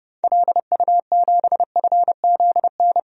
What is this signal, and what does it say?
Morse code, heard as a keyed tone, about 30 wpm, 700 Hz. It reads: LU7FZN
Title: DXCC-100 Call Signs - 6 Characters - Repeated 2x - 30wpm